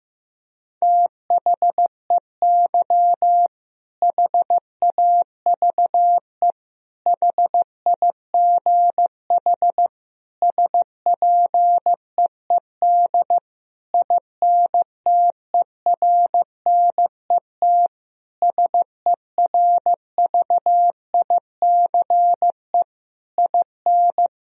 Morse code, heard as a keyed tone, about 15 wpm, 700 Hz. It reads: THEY HAVE HIGH SPEED INTERNET SERVICE IN
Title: Sentences from Top 1000 Words - Review Part 4 - 15wpm